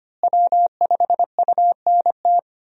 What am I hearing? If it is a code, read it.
Morse code, heard as a keyed tone, about 25 wpm, 700 Hz. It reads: W5UNT
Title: USA Call Signs - 5 Characters - Repeated 3x - 25wpm